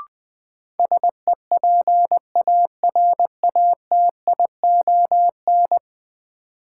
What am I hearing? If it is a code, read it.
Morse code, heard as a keyed tone, about 21 wpm, 700 Hz. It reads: SEPARATION
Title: Sets of 5 Words - New Word Drill - 20wpm